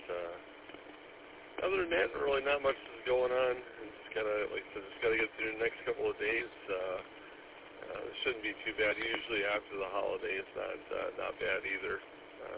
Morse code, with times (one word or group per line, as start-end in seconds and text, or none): none